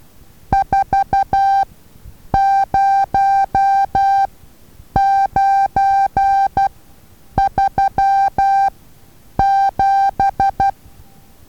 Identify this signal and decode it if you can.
Morse code, heard as a keyed tone, about 12 wpm, spaced as for 8 wpm, 790 Hz. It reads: 40937
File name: Taviro HA6VA A